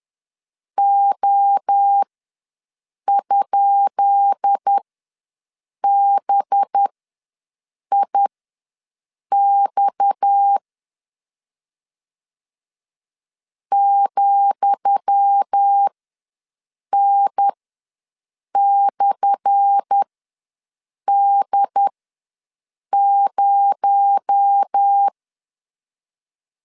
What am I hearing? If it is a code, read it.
Morse code, heard as a keyed tone, about 10 wpm, spaced as for 6 wpm, 800 Hz. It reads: O?BIX ,N/D0
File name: LEZIONI / 10 SIGNS / 10D COMA ws 10wpm 4314